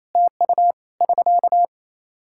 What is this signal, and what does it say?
Morse code, heard as a keyed tone, about 28 wpm, 700 Hz. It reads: TU <SK>
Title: Practice QSO - Exchange 4 - Ending - 28wpm